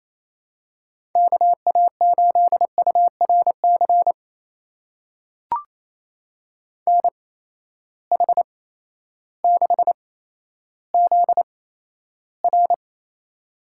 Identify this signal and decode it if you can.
Morse code, heard as a keyed tone, about 27 wpm, spaced as for 13 wpm, 700 Hz. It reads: KA8URC NH6ZR
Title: USA Call Signs - 8x Character Spacing - 28wpm